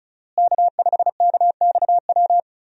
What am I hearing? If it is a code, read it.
Morse code, heard as a keyed tone, about 36 wpm, 700 Hz. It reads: K5KXW
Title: USA Call Signs - 5 Characters - Repeated 3x - 35wpm